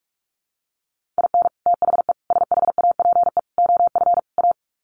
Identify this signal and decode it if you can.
Morse code, heard as a keyed tone, about 44 wpm, 700 Hz. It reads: IN THE SHAPE OF A
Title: Sets of 5 Words Encore 45wpm